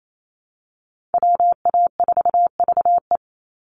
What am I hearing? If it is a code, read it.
Morse code, heard as a keyed tone, about 28 wpm, 700 Hz. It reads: WA4VE
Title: USA Call Signs 28wpm